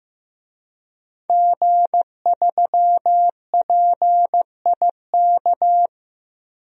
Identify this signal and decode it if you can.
Morse code, heard as a keyed tone, about 15 wpm, 700 Hz. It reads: G3PIK
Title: DXCC-100 Call Signs - 5 Characters - 15wpm